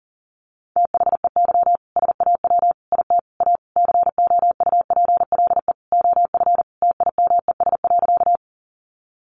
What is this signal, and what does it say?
Morse code, heard as a keyed tone, about 40 wpm, 700 Hz. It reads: THEY SAW IT A COUPLE OF TIMES.
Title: Sentences from Top 1100 Words 40wpm